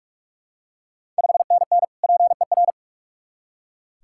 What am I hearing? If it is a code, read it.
Morse code, heard as a keyed tone, about 46 wpm, 700 Hz. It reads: 5NN PER
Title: Ontario QSO Party Exchange 45wpm